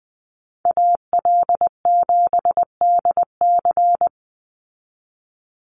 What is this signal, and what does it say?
Morse code, heard as a keyed tone, about 20 wpm, 700 Hz. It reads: AL7DC
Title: USA Call Signs 20wpm